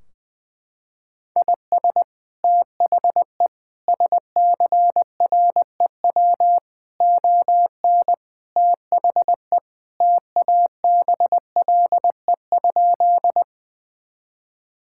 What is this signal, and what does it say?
Morse code, heard as a keyed tone, about 20 wpm, 700 Hz. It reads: IS THE SCREW ON THE TABLE?
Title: Sentences from Top 1600 Words 20wpm